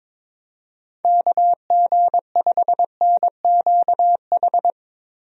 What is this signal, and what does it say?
Morse code, heard as a keyed tone, about 22 wpm, 700 Hz. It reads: KG5NQH
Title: USA Call Signs - 6 Characters - Repeated 2x - 22wpm